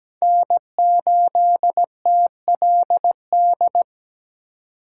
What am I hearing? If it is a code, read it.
Morse code, heard as a keyed tone, about 20 wpm, 700 Hz. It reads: N8TLD
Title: Speed Racing - USA 1x3 Call Signs 17wpm